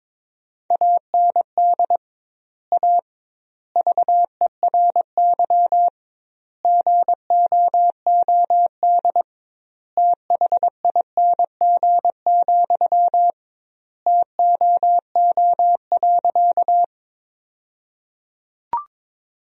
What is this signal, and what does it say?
Morse code, heard as a keyed tone, about 21 wpm, 700 Hz. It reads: AND A VERY GOOD THING, TOO.
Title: 2x Word-Spacing - Sentences from Top 200 Words - 22wpm